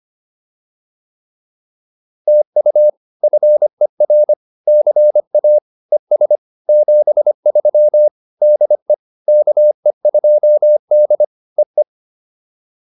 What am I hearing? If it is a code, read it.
Morse code, heard as a keyed tone, about 25 wpm, 600 Hz. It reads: TU FER CA ES 73 DE KE2D EE